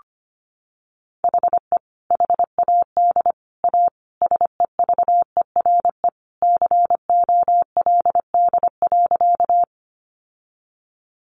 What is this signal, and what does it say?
Morse code, heard as a keyed tone, about 25 wpm, 700 Hz. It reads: HE HAD A SEVERE COLD.
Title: Sentences from Top 1400 Words 25wpm